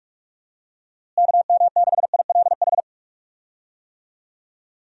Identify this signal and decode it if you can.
Morse code, heard as a keyed tone, about 42 wpm, 700 Hz. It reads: KM6ILH
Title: USA Call Signs - 6 Characters - Repeated 2x - 45wpm